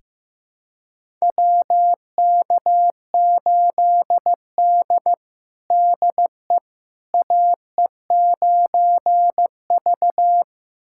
Morse code, heard as a keyed tone, about 15 wpm, 700 Hz. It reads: WK8D DE AE9V
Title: USA Callsign DE Callsign - 4 Characters - 15wpm